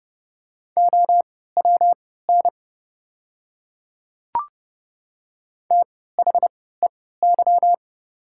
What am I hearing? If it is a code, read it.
Morse code, heard as a keyed tone, about 30 wpm, spaced as for 17 wpm, 700 Hz. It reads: OWN THEY
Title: 3x Character Spacing - Top 300 Words - 30wpm